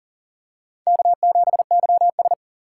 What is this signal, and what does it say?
Morse code, heard as a keyed tone, about 40 wpm, 700 Hz. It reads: K7YS
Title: USA Call Signs - 4 Characters - 3x Character Spacing - 40wpm